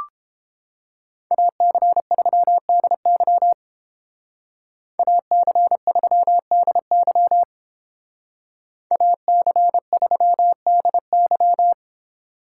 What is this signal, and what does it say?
Morse code, heard as a keyed tone, about 30 wpm, 700 Hz. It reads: AC3DY AC3DY AC3DY